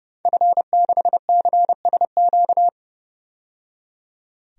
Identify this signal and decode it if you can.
Morse code, heard as a keyed tone, about 30 wpm, 700 Hz. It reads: F6CSQ
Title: DXCC-100 Call Signs - 5 Characters - 30wpm